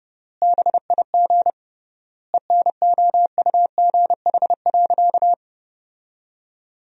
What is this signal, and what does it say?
Morse code, heard as a keyed tone, about 32 wpm, 700 Hz. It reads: BIG ENOUGH.
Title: 3x Word-Spacing - Sentences from Top 400 Words - 30wpm